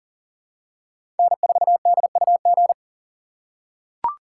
Morse code, heard as a keyed tone, about 40 wpm, 700 Hz. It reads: N4DUG
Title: USA Call Signs - 4x Character Spacing - 40wpm